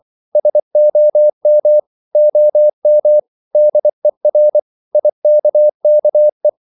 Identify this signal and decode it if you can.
Morse code, heard as a keyed tone, about 24 wpm, 600 Hz. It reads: SOM OM DER IKKE
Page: Ferie 23 wpm